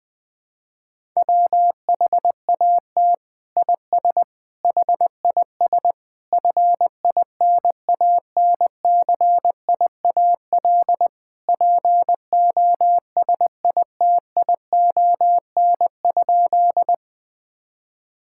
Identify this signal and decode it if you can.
Morse code, heard as a keyed tone, about 21 wpm, 700 Hz. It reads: WHAT IS HIS FINANCIAL POSITION?
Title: Sentences from Top 1000 Words - Review Part 5 - 20wpm